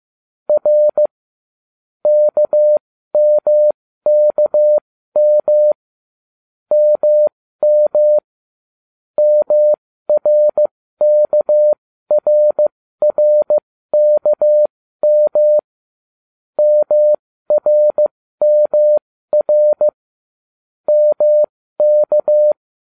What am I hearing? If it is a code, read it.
Morse code, heard as a keyed tone, about 15 wpm, spaced as for 13 wpm, 600 Hz. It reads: R KMKM MM MRKRRKM MRMR MK